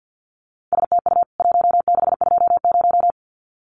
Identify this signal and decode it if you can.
Morse code, heard as a keyed tone, about 50 wpm, 700 Hz. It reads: STU 1620
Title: CWT Exchange 50wpm